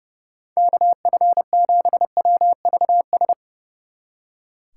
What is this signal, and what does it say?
Morse code, heard as a keyed tone, about 36 wpm, 700 Hz. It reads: KF7WVS
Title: Speed Racing - USA 2x3 Call Signs 30wpm